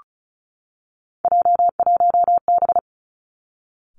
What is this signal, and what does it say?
Morse code, heard as a keyed tone, about 36 wpm, 700 Hz. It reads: J1B
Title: Three Challenging Letter-Number Combinations 35wpm